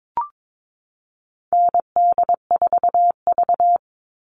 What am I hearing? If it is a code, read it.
Morse code, heard as a keyed tone, about 22 wpm, 700 Hz. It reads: ND4V
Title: Call Signs Repeated 2x - 22wpm